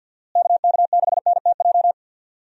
Morse code, heard as a keyed tone, about 52 wpm, 700 Hz. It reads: KK6NTJ
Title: USA Call Signs - 6 Characters - Repeated 3x - 50wpm